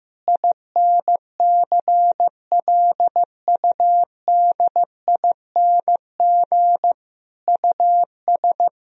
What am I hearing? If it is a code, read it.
Morse code, heard as a keyed tone, about 15 wpm, 700 Hz. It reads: INCLUDING US
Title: Sentences from Top 1000 Words - Review Part 4 - 15wpm